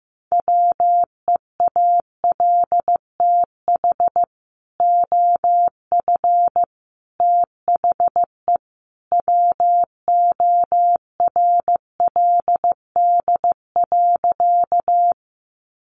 Morse code, heard as a keyed tone, about 15 wpm, 700 Hz. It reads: WEALTH OF THE WORLD.
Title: Sentences from Top 1200 Words 15wpm